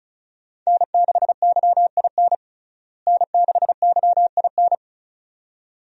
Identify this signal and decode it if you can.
Morse code, heard as a keyed tone, about 36 wpm, 700 Hz. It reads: N6YIN N6YIN
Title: USA Call Signs - 5 Characters - Repeated 2x - 35wpm